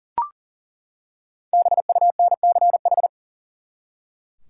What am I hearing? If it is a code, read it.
Morse code, heard as a keyed tone, about 40 wpm, 700 Hz. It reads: BUNCH
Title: Five Letter Words - 40wpm